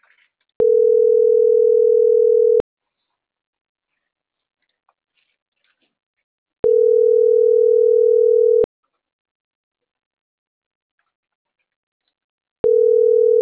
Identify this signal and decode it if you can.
Ringback tone (North America), 440 + 480 Hz.